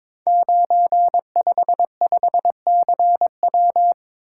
Morse code, heard as a keyed tone, about 22 wpm, 700 Hz. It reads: 955CW
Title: Code Groups - 5 Letter:Numbers - 22wpm